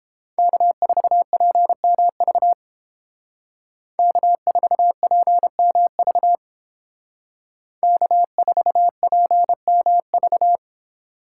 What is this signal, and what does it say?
Morse code, heard as a keyed tone, about 29 wpm, 700 Hz. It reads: K4PMV K4PMV K4PMV